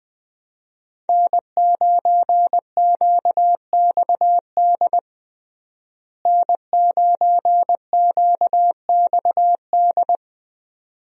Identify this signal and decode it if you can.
Morse code, heard as a keyed tone, about 20 wpm, 700 Hz. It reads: N9QXD N9QXD
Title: USA Call Signs - 5 Characters - Repeated 2x - 20wpm